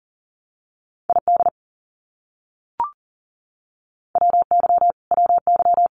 Morse code, heard as a keyed tone, about 40 wpm, 700 Hz. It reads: ID WY WY
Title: US States Repeated 3x 40wpm